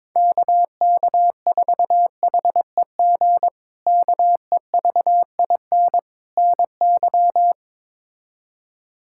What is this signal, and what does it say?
Morse code, heard as a keyed tone, about 22 wpm, 700 Hz. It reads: KK4HEG KEVIN NY
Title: K1USN SST QSO Practice - 22wpm